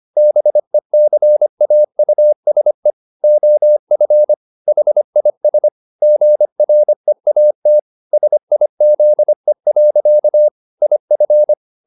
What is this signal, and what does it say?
Morse code, heard as a keyed tone, about 25 wpm, 600 Hz. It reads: BECAUSE OF HIS GREAT SIZE. IF
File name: the old man and the sea part 1 25wpm